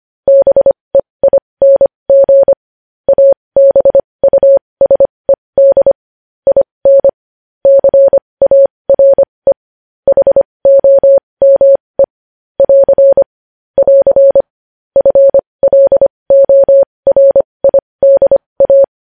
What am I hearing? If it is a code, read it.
Morse code, heard as a keyed tone, about 25 wpm, spaced as for 20 wpm, 550 Hz. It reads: BEING ABUSED IN CARE HOME <AR> <AR> FLORIDA
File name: podcast / News headlines-1x-25x20